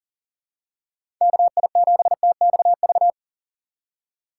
Morse code, heard as a keyed tone, about 40 wpm, 700 Hz. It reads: KI7TXV
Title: USA Call Signs - 6 Characters - 4x Character Spacing - 40wpm